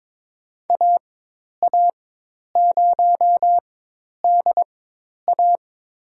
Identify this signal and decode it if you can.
Morse code, heard as a keyed tone, about 22 wpm, spaced as for 10 wpm, 700 Hz. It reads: AA0DA
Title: USA Call Signs - 4x Character Spacing - 22wpm